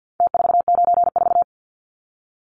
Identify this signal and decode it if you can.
Morse code, heard as a keyed tone, about 50 wpm, 700 Hz. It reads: T494
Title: Numbers - Contest Style - 4-Digit 50wpm